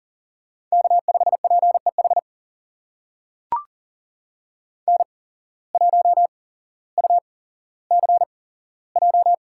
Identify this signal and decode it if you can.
Morse code, heard as a keyed tone, about 39 wpm, spaced as for 17 wpm, 700 Hz. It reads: K5PEH N1UCJ
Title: USA Call Signs - 5 Characters - 8x Character Spacing - 40wpm